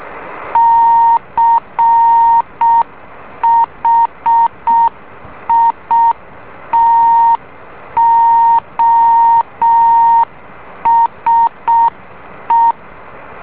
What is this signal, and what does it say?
Morse code, heard as a keyed tone, about 6 wpm, 920 Hz. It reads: CHITOSE
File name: beacon / JH8ZND